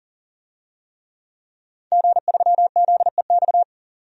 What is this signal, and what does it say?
Morse code, heard as a keyed tone, about 39 wpm, 700 Hz. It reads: G3ZEX
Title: DXCC-100 Call Signs - 5 Characters - 40wpm